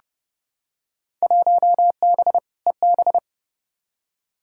Morse code, heard as a keyed tone, about 30 wpm, 700 Hz. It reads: 1B EB
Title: Field Day Exchanges - 30wpm